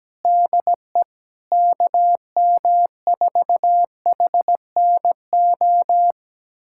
Morse code, heard as a keyed tone, about 17 wpm, 700 Hz. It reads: DE KM4HNO
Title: USA Callsign DE Callsign - 6 Characters - 17wpm